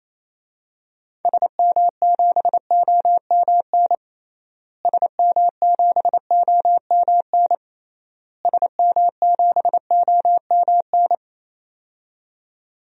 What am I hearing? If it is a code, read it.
Morse code, heard as a keyed tone, about 28 wpm, 700 Hz. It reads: SM7OMN SM7OMN SM7OMN
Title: DXCC-100 Call Signs - 6 Characters - Repeated 3x - 28wpm